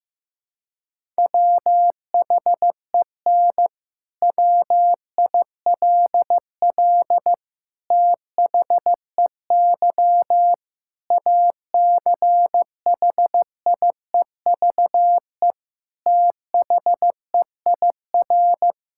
Morse code, heard as a keyed tone, about 16 wpm, 700 Hz. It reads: WHEN WILL THEY ACHIEVE THEIR
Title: Sentences from Top 1000 Words - Review Part 6 - 15wpm